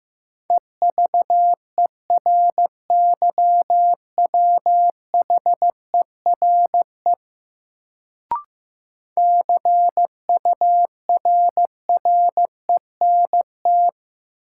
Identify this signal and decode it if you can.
Morse code, heard as a keyed tone, about 15 wpm, 700 Hz. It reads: EVERYWHERE CURRENT
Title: Sentences from Top 1600 Words - New Word Drill Alt - 15wpm